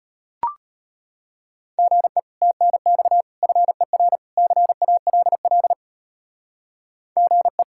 Morse code, heard as a keyed tone, about 32 wpm, 700 Hz. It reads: GE TNX FER CALL GE